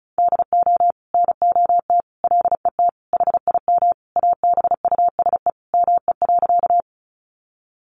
Morse code, heard as a keyed tone, about 34 wpm, 700 Hz. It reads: DO NOT LET HIM ABUSE ME.
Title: Sentences from Top 1300 Words 35wpm